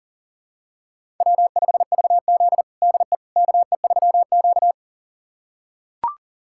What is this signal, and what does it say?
Morse code, heard as a keyed tone, about 40 wpm, 700 Hz. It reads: W5VZ DE KE3Q
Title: USA Callsign DE Callsign - 4 Characters - 40wpm